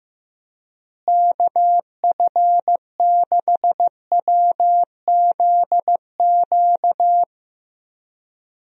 Morse code, heard as a keyed tone, about 16 wpm, 700 Hz. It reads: KF6WZQ
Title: USA Call Signs - 6 Characters - Repeated 3x - 15wpm